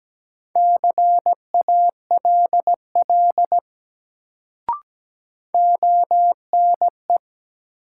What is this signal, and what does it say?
Morse code, heard as a keyed tone, about 17 wpm, 700 Hz. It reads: CALL ONE
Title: Sentences from Top 100 Words - New Word Drill - 17wpm